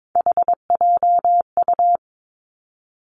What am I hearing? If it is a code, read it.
Morse code, heard as a keyed tone, about 22 wpm, 700 Hz. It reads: HJU
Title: Three Challenging Letter-Number Combinations 22wpm